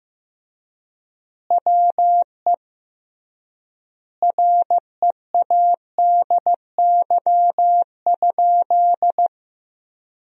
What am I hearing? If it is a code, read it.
Morse code, heard as a keyed tone, about 15 wpm, 700 Hz. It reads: WE READY?
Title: 3x Word-Spacing - Sentences from Top 400 Words - 15wpm